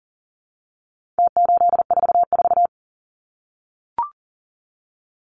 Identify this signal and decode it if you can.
Morse code, heard as a keyed tone, about 40 wpm, 700 Hz. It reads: T844
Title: Numbers - Contest Style - 4-Digit 40wpm